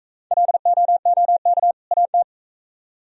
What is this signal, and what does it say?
Morse code, heard as a keyed tone, about 44 wpm, 700 Hz. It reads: LOOK AT